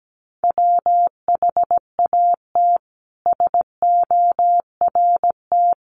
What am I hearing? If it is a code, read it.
Morse code, heard as a keyed tone, about 17 wpm, 700 Hz. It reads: WHAT SORT
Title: Sentences from Top 1300 Words 17wpm